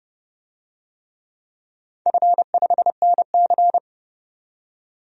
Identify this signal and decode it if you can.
Morse code, heard as a keyed tone, about 30 wpm, 700 Hz. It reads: F5NC
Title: DXCC-100 Call Signs - 4 Characters - 30wpm